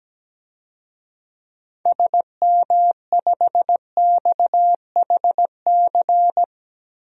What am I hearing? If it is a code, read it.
Morse code, heard as a keyed tone, about 17 wpm, 700 Hz. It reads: SM5XHC
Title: DXCC-100 Call Signs - 6 Characters - 17wpm